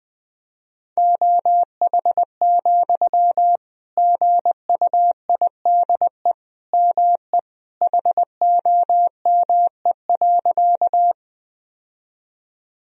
Morse code, heard as a keyed tone, about 20 wpm, 700 Hz. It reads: OH, GUIDE ME HOME.
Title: Sentences from Top 1000 Words - Review Part 3 - 20wpm